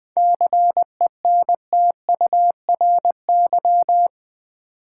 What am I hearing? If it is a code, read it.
Morse code, heard as a keyed tone, about 20 wpm, 690 Hz. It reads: CENTURY
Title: 7 Letter Words - First 3 Letters Given - 20wpm